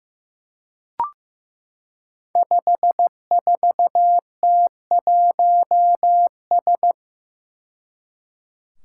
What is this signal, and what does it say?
Morse code, heard as a keyed tone, about 15 wpm, 700 Hz. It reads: 54T1S
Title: Five Challenging Letter-Number Combinations 15wpm